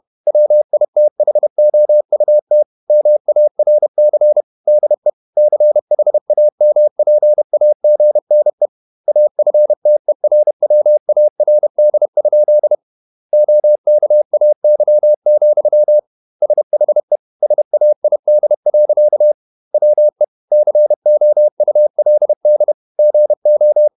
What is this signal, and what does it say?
Morse code, heard as a keyed tone, about 31 wpm, 600 Hz. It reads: WITHOUT MARC DE CHAMPAGNE AFTERWARD? OKAY, SHE SAID. WE COULD GO
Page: Never 31wpm | CW med Gnister